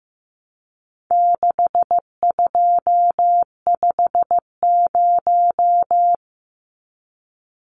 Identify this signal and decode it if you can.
Morse code, heard as a keyed tone, about 15 wpm, 700 Hz. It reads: 6250
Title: Numbers - 4-Digit 15wpm